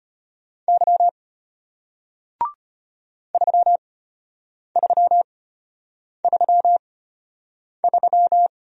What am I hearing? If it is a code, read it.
Morse code, heard as a keyed tone, about 33 wpm, spaced as for 8 wpm, 700 Hz. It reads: Y 3333